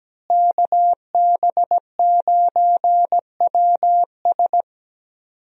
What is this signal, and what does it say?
Morse code, heard as a keyed tone, about 17 wpm, 700 Hz. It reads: KB9WS
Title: USA Call Signs - 5 Characters - Repeated 2x - 17wpm